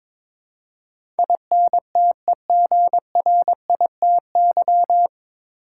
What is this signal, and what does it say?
Morse code, heard as a keyed tone, about 22 wpm, 700 Hz. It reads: INTEGRITY
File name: 9 Letter Words - First 3 Letters Given - 22wpm